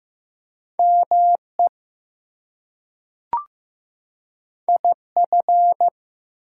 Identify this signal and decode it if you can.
Morse code, heard as a keyed tone, about 15 wpm, 700 Hz. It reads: ME IF
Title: Two Letter Words - 15wpm